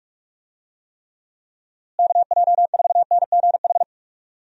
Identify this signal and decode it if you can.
Morse code, heard as a keyed tone, about 46 wpm, 700 Hz. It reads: KJ4NGH
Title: USA Call Signs - 6 Characters - Repeated 2x - 45wpm